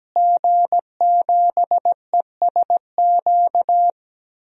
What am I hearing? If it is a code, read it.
Morse code, heard as a keyed tone, about 17 wpm, 700 Hz. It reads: G7ESQ
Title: DXCC-100 Call Signs - 5 Characters - Repeated 3x - 17wpm